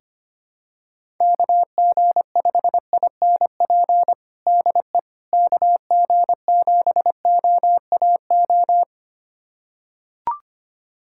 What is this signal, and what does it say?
Morse code, heard as a keyed tone, about 25 wpm, 700 Hz. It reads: KG5INP DE KG7OAO
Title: USA Callsign DE Callsign - 6 Characters - 25wpm